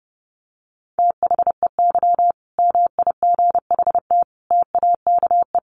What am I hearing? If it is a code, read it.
Morse code, heard as a keyed tone, about 30 wpm, 700 Hz. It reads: THEY MIGHT TAKE
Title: Sentences from Top 1000 Words 30wpm